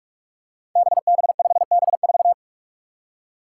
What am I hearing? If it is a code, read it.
Morse code, heard as a keyed tone, about 44 wpm, 700 Hz. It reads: BB5B4
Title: Five Challenging Letter-Number Combinations 45wpm